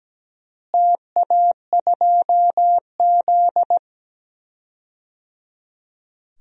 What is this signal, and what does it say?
Morse code, heard as a keyed tone, about 17 wpm, 700 Hz. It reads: TA2Z
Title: DXCC-100 Call Signs - 4 Characters - Repeated 3x - 17wpm